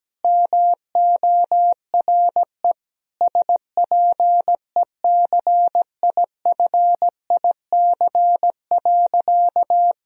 Morse code, heard as a keyed tone, about 17 wpm, 700 Hz. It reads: MORE SPECIFIC.
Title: Sentences from Top 1000 Words - Review Part 3 - 17wpm